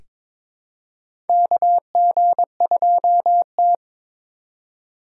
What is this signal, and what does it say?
Morse code, heard as a keyed tone, about 22 wpm, 700 Hz. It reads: KG2T
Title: USA Call Signs - 4 Characters - 3x Character Spacing - 22wpm